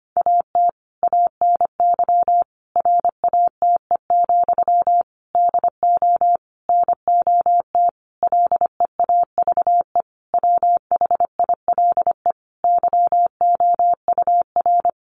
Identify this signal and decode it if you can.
Morse code, heard as a keyed tone, about 25 wpm, 700 Hz. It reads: AT ANY RATE, DO NOT LEAVE WHILE YOUR
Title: Sentences from Top 1500 Words 25wpm